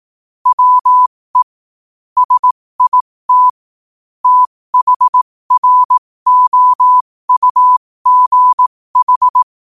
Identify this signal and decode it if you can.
Morse code, heard as a keyed tone, about 18 wpm, spaced as for 15 wpm, 1000 Hz. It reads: WE SIT THROUGH
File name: qotdfiles / 15wpm